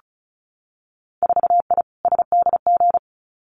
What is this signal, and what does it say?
Morse code, heard as a keyed tone, about 36 wpm, 700 Hz. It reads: 4I SDG
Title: Winter Field Day Exchanges Encore 35wpm